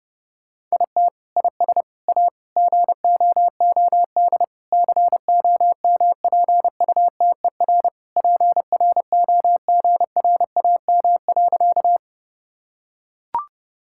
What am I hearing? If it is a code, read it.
Morse code, heard as a keyed tone, about 30 wpm, 700 Hz. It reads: IT IS A GOOD COMPUTER PROGRAM.
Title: Sentences from Top 1000 Words - Review Part 5 - 30wpm